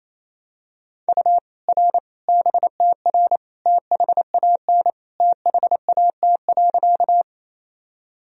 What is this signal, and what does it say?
Morse code, heard as a keyed tone, about 27 wpm, 700 Hz. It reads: U R BTR THAN THAT.
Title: Sentences from Top 400 Words and Abbreviations 28wpm